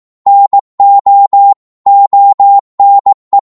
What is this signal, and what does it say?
Morse code, heard as a keyed tone, about 18 wpm, 800 Hz. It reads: NO ONE